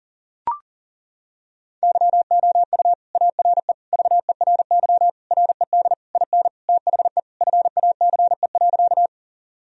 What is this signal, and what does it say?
Morse code, heard as a keyed tone, about 40 wpm, 700 Hz. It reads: YOU ARE VERY RED IN THE FACE.
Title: Sentences from Top 1000 Words - Review Part 2 - 40wpm